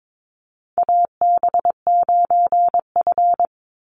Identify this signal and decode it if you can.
Morse code, heard as a keyed tone, about 22 wpm, 700 Hz. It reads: AB9F
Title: USA Call Signs 22wpm